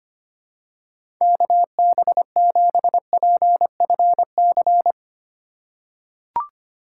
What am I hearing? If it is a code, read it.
Morse code, heard as a keyed tone, about 25 wpm, 700 Hz. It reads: KB7PFC
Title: USA Call Signs - 6 Characters - 8x Character Spacing - 25wpm